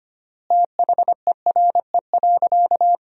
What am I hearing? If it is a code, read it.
Morse code, heard as a keyed tone, about 25 wpm, 700 Hz. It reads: THERE.
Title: 3x Word-Spacing - Sentences from Top 500 Words - 25wpm